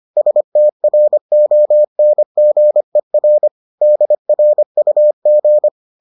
Morse code, heard as a keyed tone, about 25 wpm, 600 Hz. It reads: STRONGER DRUG